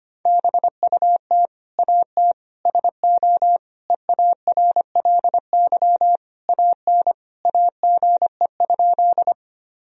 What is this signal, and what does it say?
Morse code, heard as a keyed tone, about 25 wpm, 700 Hz. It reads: BUT AT SO EARLY AN AGE?
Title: Sentences from Top 1000 Words - Review Part 3 - 25wpm